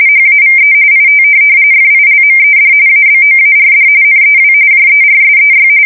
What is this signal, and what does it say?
RTTY, about 45.45 baud. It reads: HAVE IT FILMED ABOARD NEW JERSEY
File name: fsk170 / USS-Missouri